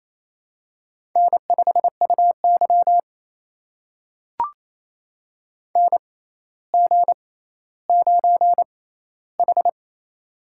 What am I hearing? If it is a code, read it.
Morse code, heard as a keyed tone, about 27 wpm, spaced as for 14 wpm, 700 Hz. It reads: N5UY NG9H
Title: USA Call Signs - 4 Characters - 6x Character Spacing - 28wpm